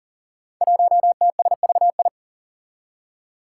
Morse code, heard as a keyed tone, about 39 wpm, 700 Hz. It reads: 1TSVI
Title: Five Challenging Letter-Number Combinations 40wpm